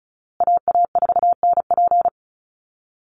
Morse code, heard as a keyed tone, about 34 wpm, 700 Hz. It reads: AA4NP
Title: Call Signs Repeated 2x - 35wpm